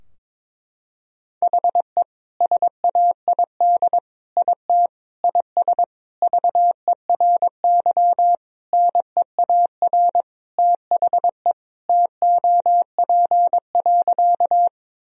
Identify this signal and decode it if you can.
Morse code, heard as a keyed tone, about 22 wpm, 700 Hz. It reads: HE SAID IT IS VERY NEAR THE TOP.
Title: Sentences from Top 1000 Words - Review Part 2 - 22wpm